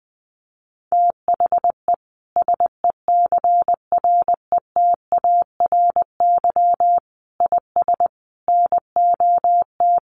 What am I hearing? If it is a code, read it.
Morse code, heard as a keyed tone, about 20 wpm, 700 Hz. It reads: THE SECRETARY IS NOT
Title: Sentences from Top 1300 Words 20wpm